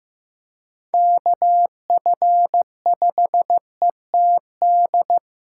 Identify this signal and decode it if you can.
Morse code, heard as a keyed tone, about 15 wpm, 700 Hz. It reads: KF5ETD
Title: K1USN SST QSO Practice - 15wpm